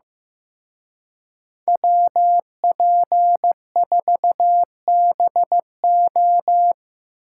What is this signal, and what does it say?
Morse code, heard as a keyed tone, about 18 wpm, 700 Hz. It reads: WP4BO
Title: Speed Racing - USA Call Signs 15wpm